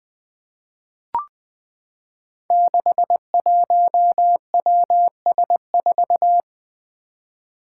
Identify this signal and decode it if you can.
Morse code, heard as a keyed tone, about 20 wpm, 700 Hz. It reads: 61WS4
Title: Five Challenging Letter-Number Combinations 20wpm